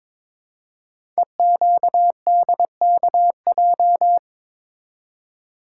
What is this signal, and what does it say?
Morse code, heard as a keyed tone, about 22 wpm, 700 Hz. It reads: EQDKJ
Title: Code Groups - 5 Letters - 22wpm